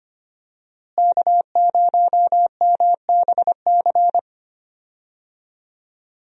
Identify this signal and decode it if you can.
Morse code, heard as a keyed tone, about 25 wpm, 700 Hz. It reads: K0MBC
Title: USA Call Signs - 5 Characters - Repeated 3x - 25wpm